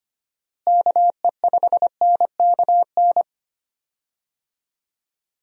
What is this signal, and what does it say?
Morse code, heard as a keyed tone, about 25 wpm, 700 Hz. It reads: KE5NKN
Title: USA Call Signs - 6 Characters - Repeated 3x - 25wpm